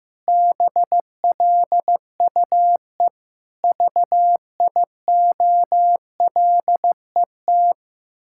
Morse code, heard as a keyed tone, about 15 wpm, 700 Hz. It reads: BLUE VIOLET
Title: Pangram Challenge v2 - 15wpm